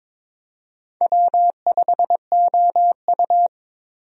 Morse code, heard as a keyed tone, about 23 wpm, 700 Hz. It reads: W5OU
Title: USA Call Signs - 4 Characters - 22wpm